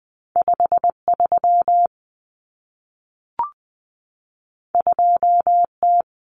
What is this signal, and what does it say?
Morse code, heard as a keyed tone, about 20 wpm, 700 Hz. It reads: 53 2T
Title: Common Cut Numbers - 2-Digit 20wpm